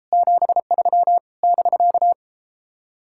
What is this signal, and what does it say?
Morse code, heard as a keyed tone, about 33 wpm, 700 Hz. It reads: 73 <BK>